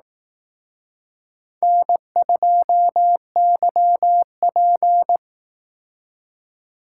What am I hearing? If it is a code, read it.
Morse code, heard as a keyed tone, about 18 wpm, 700 Hz. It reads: N2YP
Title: Speed Racing - USA 1x2 Call Signs 15wpm